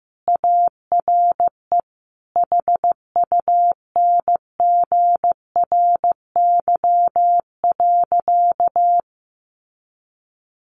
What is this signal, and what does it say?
Morse code, heard as a keyed tone, about 15 wpm, 700 Hz. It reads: ARE HUNGRY.
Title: Sentences from Top 1400 Words 15wpm